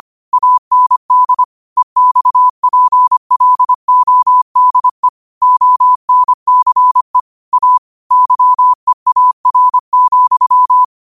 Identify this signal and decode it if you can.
Morse code, heard as a keyed tone, about 25 wpm, 1000 Hz. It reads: AND EXPLODE ONCE A YEAR,
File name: qotdfiles / 25wpm